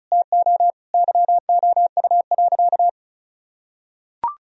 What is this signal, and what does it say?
Morse code, heard as a keyed tone, about 36 wpm, 700 Hz. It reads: TO YOU.
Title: Sentences from Top 1000 Words - Review Part 1 - 35wpm